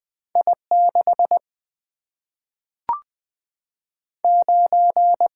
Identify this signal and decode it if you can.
Morse code, heard as a keyed tone, about 21 wpm, 700 Hz. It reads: I6 9
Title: Single-Letter Single-Number 20wpm